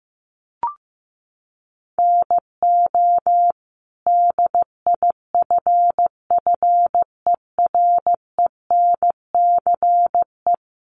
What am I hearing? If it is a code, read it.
Morse code, heard as a keyed tone, about 14 wpm, 700 Hz. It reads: NO DIFFERENCE
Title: Sentences from Top 800 Words 15wpm